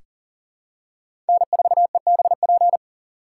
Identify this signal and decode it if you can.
Morse code, heard as a keyed tone, about 40 wpm, 700 Hz. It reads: N4EBP
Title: USA Call Signs - 3x Character Spacing - 40wpm